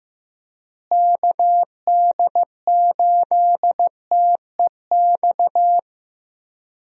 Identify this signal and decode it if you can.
Morse code, heard as a keyed tone, about 15 wpm, 700 Hz. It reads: KD8TEX
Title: USA Call Signs - 6 Characters - Repeated 3x - 15wpm